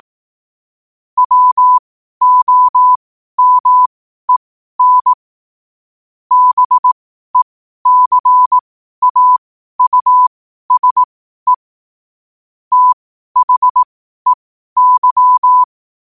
Morse code, heard as a keyed tone, about 18 wpm, spaced as for 12 wpm, 1000 Hz. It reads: WOMEN BECAUSE THEY
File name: qotdfiles / 12wpm